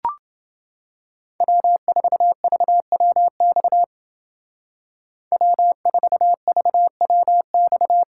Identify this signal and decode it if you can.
Morse code, heard as a keyed tone, about 29 wpm, 700 Hz. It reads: W4VWX W4VWX